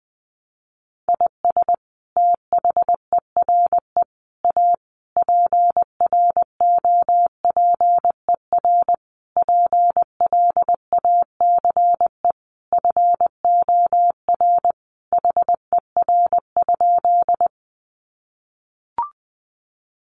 Morse code, heard as a keyed tone, about 20 wpm, 700 Hz. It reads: IS THERE A PROPER PLACE FOR HER?
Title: Sentences from Top 800 Words 20wpm